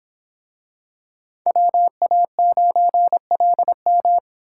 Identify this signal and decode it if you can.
Morse code, heard as a keyed tone, about 25 wpm, 700 Hz. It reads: WA9LM